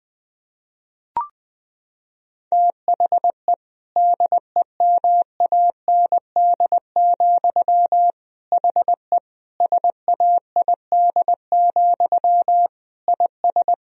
Morse code, heard as a keyed tone, about 20 wpm, 700 Hz. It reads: THE DEMAND, HE SAID, IS
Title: Sentences from Top 1000 Words - Review Part 4 - 20wpm